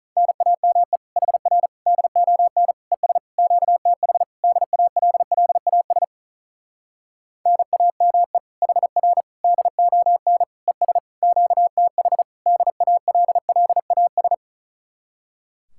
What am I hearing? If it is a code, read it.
Morse code, heard as a keyed tone, about 44 wpm, 700 Hz. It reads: NAME HR DON ES QTH DALLAS NAME HR DON ES QTH DALLAS
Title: Speed-Racing Common QSO Phrases - 35wpm